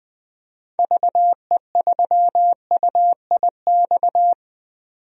Morse code, heard as a keyed tone, about 20 wpm, 700 Hz. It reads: VE3UIX
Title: DXCC-100 Call Signs - 6 Characters - 20wpm